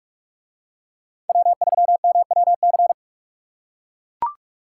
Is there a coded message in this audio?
Yes: Morse code from 1.3 to 2.9 s, heard as a keyed tone, about 42 wpm, 700 Hz. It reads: W3MWC